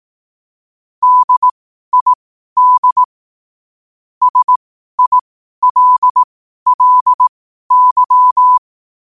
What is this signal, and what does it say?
Morse code, heard as a keyed tone, about 18 wpm, spaced as for 12 wpm, 1000 Hz. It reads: DID SILLY
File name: qotdfiles / 12wpm